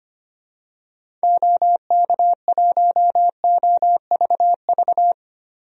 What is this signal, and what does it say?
Morse code, heard as a keyed tone, about 25 wpm, 700 Hz. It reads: OK1OVV
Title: DXCC-100 Call Signs - 6 Characters - Repeated 2x - 25wpm